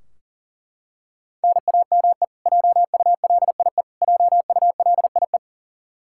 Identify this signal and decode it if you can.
Morse code, heard as a keyed tone, about 39 wpm, 700 Hz. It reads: NAME JULIE JULIE
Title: Practice QSO - Exchange 1 - Beginning - 40wpm